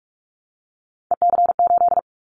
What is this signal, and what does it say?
Morse code, heard as a keyed tone, about 44 wpm, 700 Hz. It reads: EC8
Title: Callsign Prefixes for Top 100 DXCC - 45wpm